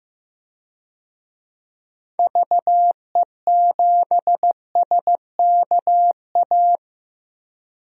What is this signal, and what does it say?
Morse code, heard as a keyed tone, about 15 wpm, 700 Hz. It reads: VE7SKA
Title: DXCC-100 Call Signs - 6 Characters - 15wpm